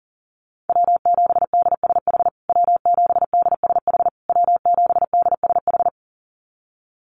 Morse code, heard as a keyed tone, about 40 wpm, 700 Hz. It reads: W7DSH W7DSH W7DSH
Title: USA Call Signs Repeated 3x 40wpm